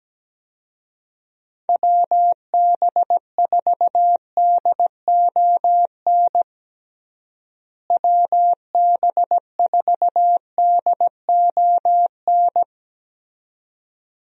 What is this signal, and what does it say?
Morse code, heard as a keyed tone, about 17 wpm, 700 Hz. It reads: WB4DON WB4DON
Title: USA Call Signs - 6 Characters - Repeated 2x - 17wpm